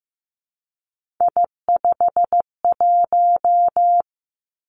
Morse code, heard as a keyed tone, about 15 wpm, 700 Hz. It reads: I51
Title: Three Challenging Letter-Number Combinations 15wpm